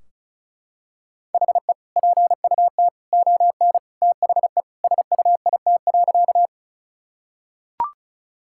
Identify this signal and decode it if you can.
Morse code, heard as a keyed tone, about 35 wpm, 700 Hz. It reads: HE PUT ON THE SUIT.
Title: Sentences from Top 1600 Words 35wpm